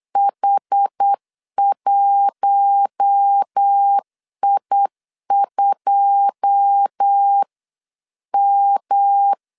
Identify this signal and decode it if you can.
Morse code, heard as a keyed tone, about 8 wpm, 800 Hz. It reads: H1I2M
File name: CW / DET / 1 9 1500C S B 8wpm 5624